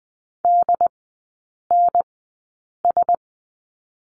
Morse code, heard as a keyed tone, about 20 wpm, spaced as for 8 wpm, 700 Hz. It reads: DNS
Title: 067 Lesson 12e - Letter d - Three Characters 20wpm